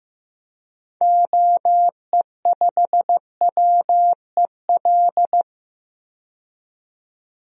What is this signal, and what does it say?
Morse code, heard as a keyed tone, about 15 wpm, 700 Hz. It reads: OE5WEL
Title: DXCC-100 Call Signs - 6 Characters - 15wpm